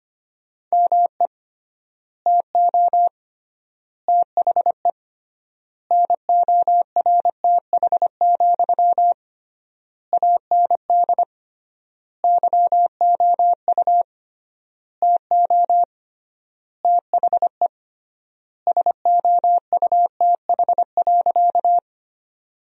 Morse code, heard as a keyed tone, about 25 wpm, 700 Hz. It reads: ME TO THE NORTH, AND YOU TO THE SOUTH.
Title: 3x Word-Spacing - Sentences from Top 400 Words - 25wpm